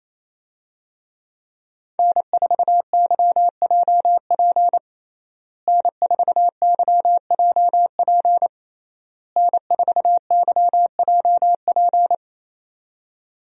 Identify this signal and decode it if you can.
Morse code, heard as a keyed tone, about 28 wpm, 700 Hz. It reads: N4YJP N4YJP N4YJP
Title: USA Call Signs - 5 Characters - Repeated 3x - 28wpm